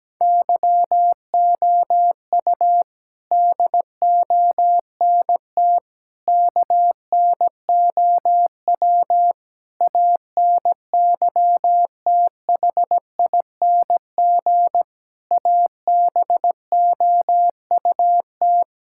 Morse code, heard as a keyed tone, about 17 wpm, 700 Hz. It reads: YOU DONT KNOW ANYTHING ABOUT
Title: Sentences from Top 1000 Words - Review Part 6 - 17wpm